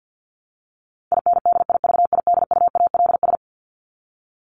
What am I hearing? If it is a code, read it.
Morse code, heard as a keyed tone, about 50 wpm, 700 Hz. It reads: INDIVIDUALS
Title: ES:S Suffix - Top 100 Words with Base Words 50wpm